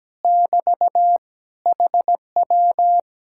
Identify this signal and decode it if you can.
Morse code, heard as a keyed tone, about 17 wpm, 700 Hz. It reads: <BT> HW
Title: Speed-Racing Common QSO Phrases Encore - 17wpm